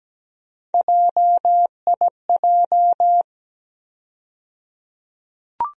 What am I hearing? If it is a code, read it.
Morse code, heard as a keyed tone, about 17 wpm, 700 Hz. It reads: JIJ
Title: USA Call Sign Postfixes - 3 Letters 17wpm